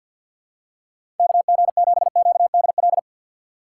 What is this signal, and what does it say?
Morse code, heard as a keyed tone, about 52 wpm, 700 Hz. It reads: KG7QDL
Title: USA Call Signs - 6 Characters - 4x Character Spacing - 50wpm